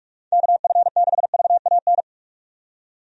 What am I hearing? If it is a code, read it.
Morse code, heard as a keyed tone, about 45 wpm, 700 Hz. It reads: KF6VAN
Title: USA Call Signs - 6 Characters - 45wpm